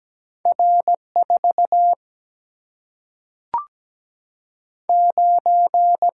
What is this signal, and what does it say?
Morse code, heard as a keyed tone, about 17 wpm, 700 Hz. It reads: R 4 9
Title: Single-Letter Single-Number 17wpm